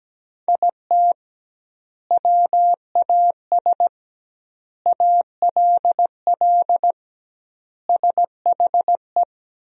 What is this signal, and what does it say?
Morse code, heard as a keyed tone, about 17 wpm, 700 Hz. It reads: IT WAS ALL SHE
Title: 2x Word-Spacing - Sentences from Top 400 Words - 17wpm